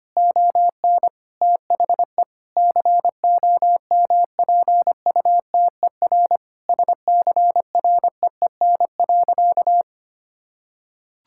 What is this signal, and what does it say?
Morse code, heard as a keyed tone, about 25 wpm, 700 Hz. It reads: ON THE COMPUTER SCREEN.
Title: Sentences from Top 1000 Words - Review Part 6 - 25wpm